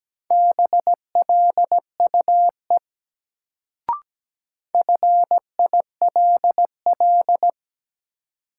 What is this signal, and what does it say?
Morse code, heard as a keyed tone, about 17 wpm, 700 Hz. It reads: BLUE FILL
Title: Sentences from Top 1000 Words - Review Part 3 - New Word Drill - 17wpm